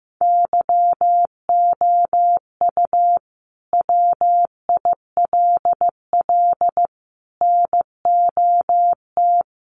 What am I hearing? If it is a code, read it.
Morse code, heard as a keyed tone, about 15 wpm, 700 Hz. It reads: YOU WILL NOT
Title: Sentences from Top 1100 Words 15wpm